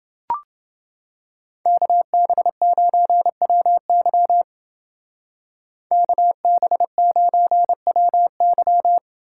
Morse code, heard as a keyed tone, about 29 wpm, 700 Hz. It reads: KB9WY KB9WY